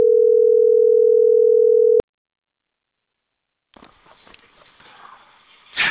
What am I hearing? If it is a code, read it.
Ringback tone (North America), 440 + 480 Hz.